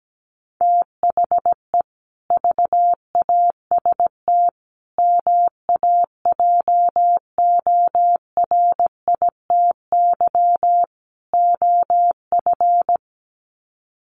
Morse code, heard as a keyed tone, about 17 wpm, 700 Hz. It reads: THE VAST MAJORITY OF
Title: Sets of 4 Words 17wpm